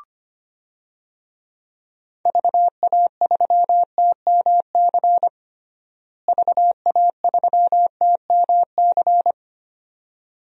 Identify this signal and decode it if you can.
Morse code, heard as a keyed tone, about 25 wpm, 700 Hz. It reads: VA3TMC VA3TMC
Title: DXCC-100 Call Signs - 6 Characters - Repeated 2x - 25wpm